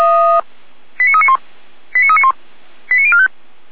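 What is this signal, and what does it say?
Five-tone selective call (ZVEI), heard as quick runs of five tones, tones of about 70 ms. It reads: 89391 89391 89045
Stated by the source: ZVEI five-tone call received by SDR, from audio.f85360.feuerwehr-orlamuende.org fire brigade server.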